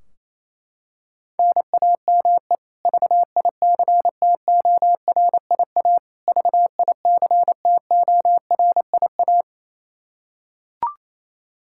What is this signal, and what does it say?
Morse code, heard as a keyed tone, about 28 wpm, 700 Hz. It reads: NAME VICTORIA VICTORIA
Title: K1USN SST QSO Practice 28wpm